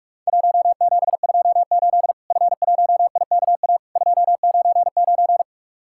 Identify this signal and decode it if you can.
Morse code, heard as a keyed tone, about 44 wpm, 700 Hz. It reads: 1728 F1IKA 299
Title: Worked All Europe DX Contest - QTC Exchange - 45wpm